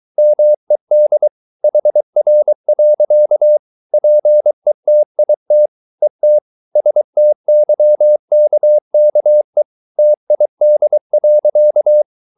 Morse code, heard as a keyed tone, about 23 wpm, 600 Hz. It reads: MED HR. PETIT ET STYKKE TID.
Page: Ferie 23 wpm